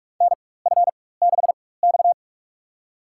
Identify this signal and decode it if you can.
Morse code, heard as a keyed tone, about 47 wpm, spaced as for 21 wpm, 700 Hz. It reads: NF6X
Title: USA Call Signs - 4 Characters - 4x Character Spacing - 45wpm